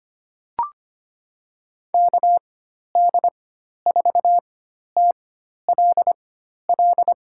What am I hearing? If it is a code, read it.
Morse code, heard as a keyed tone, about 25 wpm, spaced as for 12 wpm, 700 Hz. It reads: KD4TLL
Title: USA Call Signs - 6 Characters - 4x Character Spacing - 25wpm